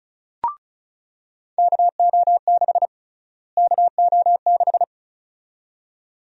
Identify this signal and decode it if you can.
Morse code, heard as a keyed tone, about 36 wpm, 700 Hz. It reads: KO6 KO6
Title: USA Call Sign Prefixes - Repeated 2x - 35wpm